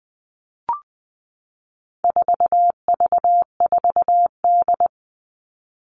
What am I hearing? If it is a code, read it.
Morse code, heard as a keyed tone, about 20 wpm, 700 Hz. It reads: 4V4D
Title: All Cut Numbers - 4-Digit 20wpm